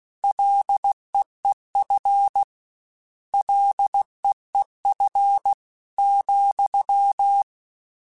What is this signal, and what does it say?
Morse code, heard as a keyed tone, about 16 wpm, 800 Hz. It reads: LEEF LEEF ,
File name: Short Runs / Short Runs-mp3s / SinglePrefixes @15